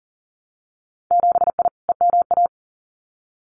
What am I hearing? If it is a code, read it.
Morse code, heard as a keyed tone, about 40 wpm, 700 Hz. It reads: 7I EMA
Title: Winter Field Day Exchanges Encore 40wpm